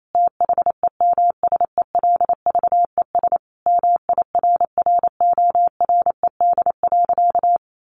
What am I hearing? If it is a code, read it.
Morse code, heard as a keyed tone, about 28 wpm, 700 Hz. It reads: THEMSELVES MIRRORED.
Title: A Sound Of Thunder 28wpm